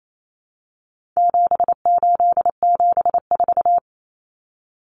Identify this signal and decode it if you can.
Morse code, heard as a keyed tone, about 28 wpm, 700 Hz. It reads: 7874
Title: Common Cut Numbers - 4-Digit 28wpm